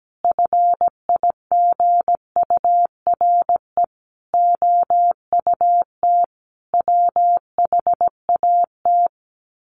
Morse code, heard as a keyed tone, about 17 wpm, 700 Hz. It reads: FIGURE OUT WHAT
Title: Sets of 5 Words 17wpm